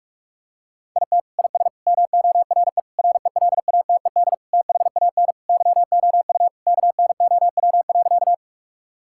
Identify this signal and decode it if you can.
Morse code, heard as a keyed tone, about 46 wpm, 700 Hz. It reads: IT IS MORE RELATED THAN YOU KNOW.
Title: Sentences from Top 1000 Words - Review Part 5 - 45wpm